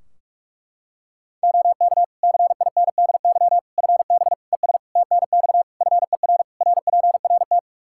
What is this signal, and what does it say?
Morse code, heard as a keyed tone, about 46 wpm, 700 Hz. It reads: OK CINDY FB ES TNX FER RPRT
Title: K1USN SST QSO Practice 45wpm